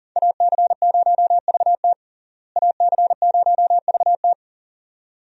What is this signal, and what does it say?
Morse code, heard as a keyed tone, about 40 wpm, 700 Hz. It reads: AC0VT AC0VT
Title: USA Call Signs - 5 Characters - Repeated 2x - 40wpm